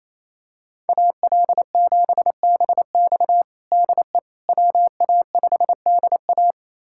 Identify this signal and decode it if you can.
Morse code, heard as a keyed tone, about 28 wpm, 700 Hz. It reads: AL7BX DE WA5DA
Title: USA Callsign DE Callsign - 5 Characters - 28wpm